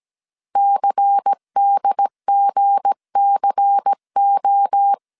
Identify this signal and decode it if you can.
Morse code, heard as a keyed tone, about 17 wpm, 800 Hz. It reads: CDGCO
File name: CW / DET / 1 9 1500C S A 16wpm 2828